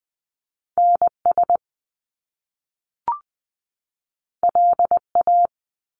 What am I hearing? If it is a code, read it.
Morse code, heard as a keyed tone, about 20 wpm, 700 Hz. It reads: NS LA
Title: Top Two Letter Combinations in Words 20wpm